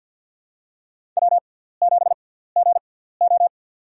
Morse code, heard as a keyed tone, about 49 wpm, spaced as for 17 wpm, 700 Hz. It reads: W7GO
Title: USA Call Signs - 6x Character Spacing - 50wpm